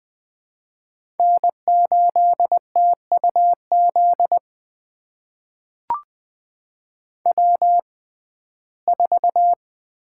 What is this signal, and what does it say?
Morse code, heard as a keyed tone, about 20 wpm, spaced as for 14 wpm, 700 Hz. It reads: N8TUZ W4
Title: USA Call Signs - 5 Characters - 6x Character Spacing - 20wpm